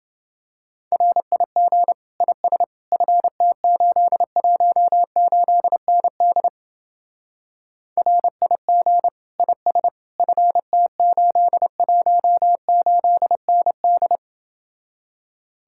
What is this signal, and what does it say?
Morse code, heard as a keyed tone, about 29 wpm, 700 Hz. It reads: RIG IS FT818ND RIG IS FT818ND